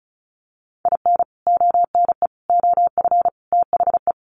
Morse code, heard as a keyed tone, about 34 wpm, 700 Hz. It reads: IN ONE OF THE
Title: Sets of 4 Words 35wpm